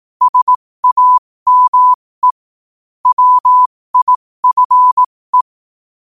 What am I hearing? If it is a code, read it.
Morse code, heard as a keyed tone, about 18 wpm, spaced as for 15 wpm, 1000 Hz. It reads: SAME WIFE
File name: qotdfiles / 15wpm